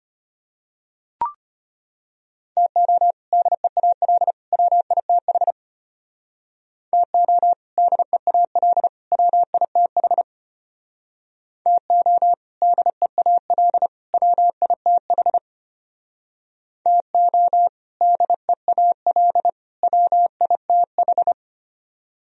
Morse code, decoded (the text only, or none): TO DEAL WITH TO DEAL WITH TO DEAL WITH TO DEAL WITH